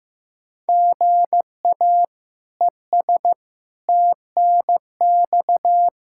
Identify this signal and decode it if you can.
Morse code, heard as a keyed tone, about 16 wpm, 700 Hz. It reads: GA ES TNX
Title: Practice QSO - Exchange 1 - Beginning - 15wpm